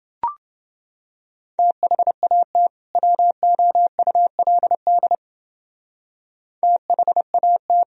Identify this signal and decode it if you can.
Morse code, heard as a keyed tone, about 31 wpm, 700 Hz. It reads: THAT WOULD THAT